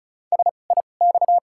Morse code, heard as a keyed tone, about 36 wpm, spaced as for 25 wpm, 700 Hz. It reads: SIX
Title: 2x Character Spacing - Top 500 Words - 35wpm